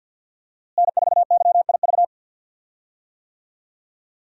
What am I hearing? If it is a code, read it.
Morse code, heard as a keyed tone, about 49 wpm, 700 Hz. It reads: N4YIV
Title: USA Call Signs - 5 Characters - Repeated 2x - 50wpm